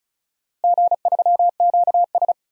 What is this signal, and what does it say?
Morse code, heard as a keyed tone, about 35 wpm, 700 Hz. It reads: G3QS
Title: DXCC-100 Call Signs - 4 Characters - Repeated 2x - 35wpm